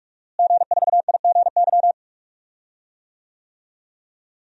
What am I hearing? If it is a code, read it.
Morse code, heard as a keyed tone, about 46 wpm, 700 Hz. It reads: G4IGY
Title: DXCC-100 Call Signs - 5 Characters - Repeated 2x - 45wpm